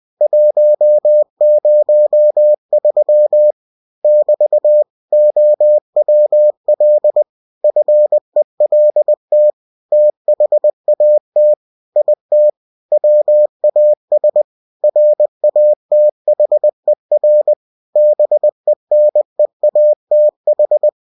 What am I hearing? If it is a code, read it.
Morse code, heard as a keyed tone, about 20 wpm, 600 Hz. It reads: 103 <BT> OWL FELT THAT IT WAS RATHER BENEATH
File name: cw-books / winnie-the-pooh / winnie-20wpm-page-0103